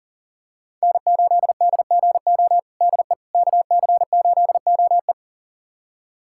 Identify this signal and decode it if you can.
Morse code, heard as a keyed tone, about 40 wpm, 700 Hz. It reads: N8DGO DE KC8OE
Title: USA Callsign DE Callsign - 5 Characters - 40wpm